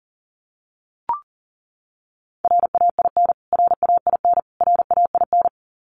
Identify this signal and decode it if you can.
Morse code, heard as a keyed tone, about 40 wpm, 700 Hz. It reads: RAIN RAIN RAIN
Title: Top 500 Words Repeated 40wpm